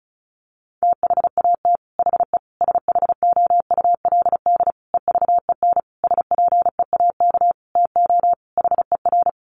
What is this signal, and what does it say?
Morse code, heard as a keyed tone, about 34 wpm, 700 Hz. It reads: THAT HE SHOULD EVEN SPEAK TO HER
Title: Sentences from Top 1500 Words 35wpm